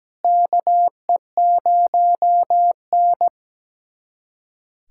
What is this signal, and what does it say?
Morse code, heard as a keyed tone, about 17 wpm, 700 Hz. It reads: KE0N
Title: USA Callsign DE Callsign - 4 Characters - 17wpm